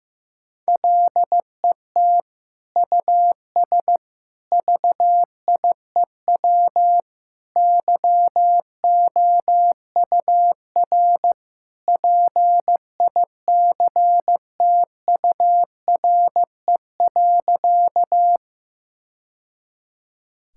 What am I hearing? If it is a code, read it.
Morse code, heard as a keyed tone, about 15 wpm, 700 Hz. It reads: LET US VIEW YOUR PICTURE.
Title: Sentences from Top 1000 Words - Review Part 6 - 15wpm